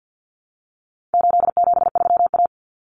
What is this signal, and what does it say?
Morse code, heard as a keyed tone, about 50 wpm, 700 Hz. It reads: 873A
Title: Common Cut Numbers - 4-Digit 50wpm